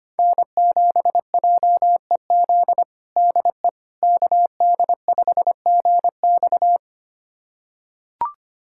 Morse code, heard as a keyed tone, about 25 wpm, 700 Hz. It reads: N7JEZ DE KD5GX
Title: USA Callsign DE Callsign - 5 Characters - 25wpm